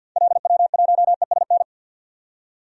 Morse code, heard as a keyed tone, about 46 wpm, 700 Hz. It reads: LW1ESN
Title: DXCC-100 Call Signs - 6 Characters - 50wpm